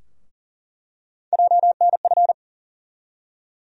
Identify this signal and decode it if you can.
Morse code, heard as a keyed tone, about 40 wpm, 700 Hz. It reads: JNF
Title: USA Call Sign Postfixes - 3 Letters 40wpm